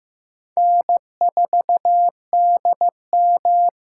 Morse code, heard as a keyed tone, about 15 wpm, 700 Hz. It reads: N4DM
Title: USA Call Signs - 5 Characters - 6x Character Spacing - 15wpm